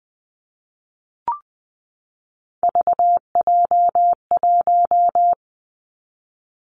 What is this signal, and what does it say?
Morse code, heard as a keyed tone, about 20 wpm, 700 Hz. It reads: VJ1
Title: Three Challenging Letter-Number Combinations 20wpm